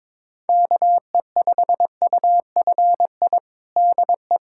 Morse code, heard as a keyed tone, about 22 wpm, 700 Hz. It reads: KE5UFI DE
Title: USA Callsign DE Callsign 22wpm